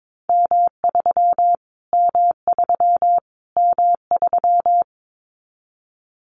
Morse code, heard as a keyed tone, about 22 wpm, 700 Hz. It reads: M3 M3 M3
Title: Callsign Prefixes for Top 25 DXCC - Repeated 3x - 22wpm